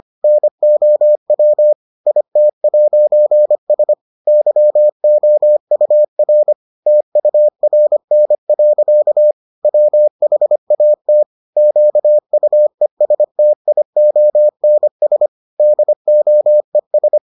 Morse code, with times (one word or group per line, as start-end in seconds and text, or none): 0.2-1.7 NOW
2.1-3.9 IT'S
4.3-6.5 YOUR
6.9-9.3 TURN.
9.6-11.2 WHAT
11.6-15.3 QUESTIONS
15.6-17.2 DOES